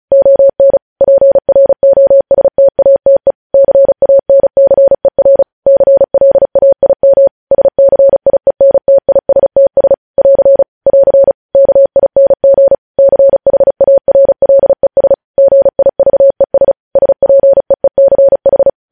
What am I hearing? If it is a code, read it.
Morse code, heard as a keyed tone, about 36 wpm, 550 Hz. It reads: ON PROSTATE CANCER CLAIM SCIENTISTS <AR> <AR> KING CHARLES GIVES SPEECH
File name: podcast / News headlines-1x-35x35